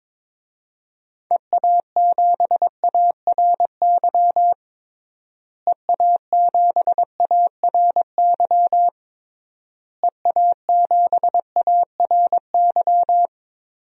Morse code, heard as a keyed tone, about 21 wpm, 700 Hz. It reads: EA7ARY EA7ARY EA7ARY
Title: DXCC-100 Call Signs - 6 Characters - Repeated 3x - 22wpm